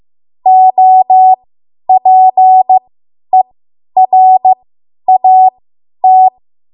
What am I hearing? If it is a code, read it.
Morse code, heard as a keyed tone, about 15 wpm, spaced as for 10 wpm, 750 Hz. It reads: OPERAT
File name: Archive / 10 WPM / 10WPM